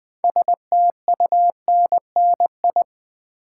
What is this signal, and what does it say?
Morse code, heard as a keyed tone, about 20 wpm, 700 Hz. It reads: STUNNI
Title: Build A Word - 3 Letters At A Time - 8-Letter Words - 20wpm